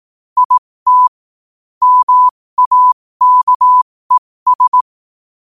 Morse code, heard as a keyed tone, about 18 wpm, spaced as for 15 wpm, 1000 Hz. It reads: IT MAKES
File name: qotdfiles / 15wpm